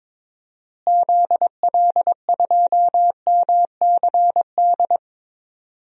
Morse code, heard as a keyed tone, about 22 wpm, 700 Hz. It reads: ZL2MCD
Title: DXCC-100 Call Signs - 6 Characters - Repeated 2x - 22wpm